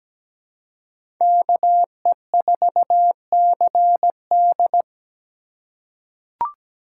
Morse code, heard as a keyed tone, about 17 wpm, 700 Hz. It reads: KE4CD
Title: USA Call Signs - 5 Characters - 3x Character Spacing - 17wpm